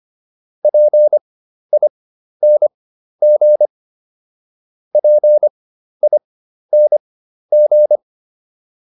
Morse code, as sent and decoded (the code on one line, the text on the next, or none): .--. .. -. --. / .--. .. -. --.
PING PING